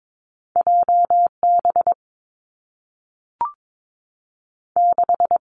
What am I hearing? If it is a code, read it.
Morse code, heard as a keyed tone, about 22 wpm, 700 Hz. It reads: J B 6
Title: Two Challenging Letter-Number Combinations 22wpm